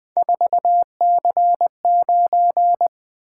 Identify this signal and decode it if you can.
Morse code, heard as a keyed tone, about 20 wpm, 700 Hz. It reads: 4C9
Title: Three Letter-Number Combinations - 20wpm